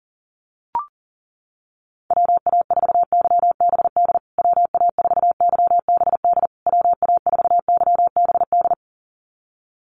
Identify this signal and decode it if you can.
Morse code, heard as a keyed tone, about 40 wpm, 700 Hz. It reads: WA4YBD WA4YBD WA4YBD
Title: USA Call Signs Repeated 3x 40wpm